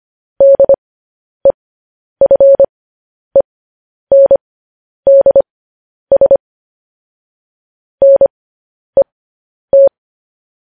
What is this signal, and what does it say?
Morse code, heard as a keyed tone, about 25 wpm, spaced as for 10 wpm, 550 Hz. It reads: DEFENDS NET
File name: podcast / News headlines-1x-25x10